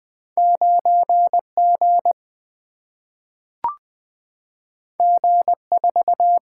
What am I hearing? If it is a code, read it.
Morse code, heard as a keyed tone, about 20 wpm, 700 Hz. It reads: 9G G4
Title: Single-Letter Single-Number 20wpm